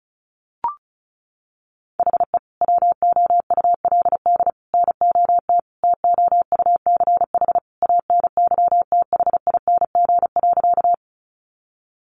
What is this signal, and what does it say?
Morse code, heard as a keyed tone, about 34 wpm, 700 Hz. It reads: HE WOULD NOT TOUCH ANYTHING.
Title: Sentences from Top 1200 Words 35wpm